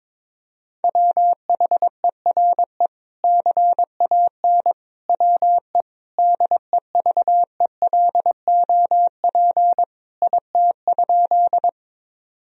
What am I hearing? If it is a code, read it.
Morse code, heard as a keyed tone, about 21 wpm, 700 Hz. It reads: WHERE CAN WE DEVELOP IT?
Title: Sentences from Top 1000 Words - Review Part 2 - 22wpm